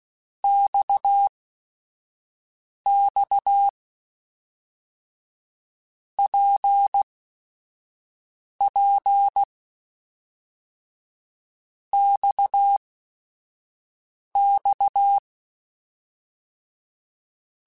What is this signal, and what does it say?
Morse code, heard as a keyed tone, about 16 wpm, spaced as for 5 wpm, 800 Hz. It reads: XX PP XX
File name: Basic Sets / Basic Letter Groups Collective7 / RKPX@5